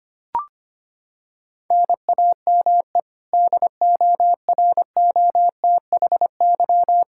Morse code, heard as a keyed tone, about 25 wpm, 700 Hz. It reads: NAME DOROTHY
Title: Practice QSO - Exchange 1 - Beginning - 25wpm